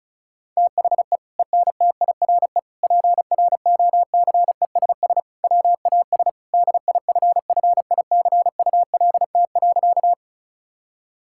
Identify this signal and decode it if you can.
Morse code, heard as a keyed tone, about 36 wpm, 700 Hz. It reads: THE ENTIRE PROCESS WAS DIFFICULT.
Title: Sentences from Top 1000 Words - Review Part 5 - 35wpm